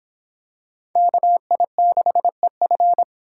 Morse code, heard as a keyed tone, about 25 wpm, 700 Hz. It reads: KI6EF